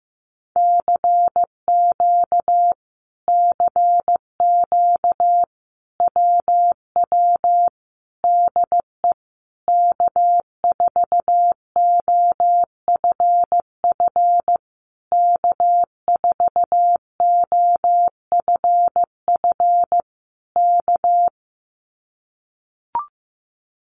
Morse code, heard as a keyed tone, about 15 wpm, 700 Hz. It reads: CQ CQ WW DE K4OFF K4OFF K
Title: Common QSO Phrases Encore 15wpm